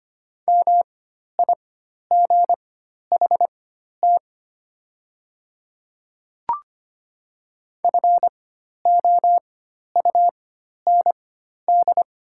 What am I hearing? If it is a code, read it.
Morse code, heard as a keyed tone, about 25 wpm, spaced as for 12 wpm, 690 Hz. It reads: MIGHT FOUND
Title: 4x Character Spacing - Top 300 Words - 25wpm